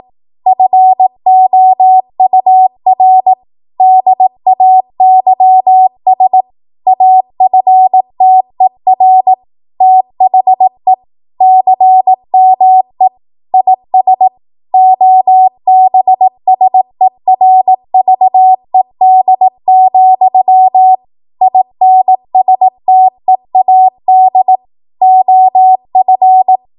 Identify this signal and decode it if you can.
Morse code, heard as a keyed tone, about 18 wpm, 750 Hz. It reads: FOUR DAYS AFTER THE CME IS OBSERVED, INSTEAD OF
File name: Morse / 18WPM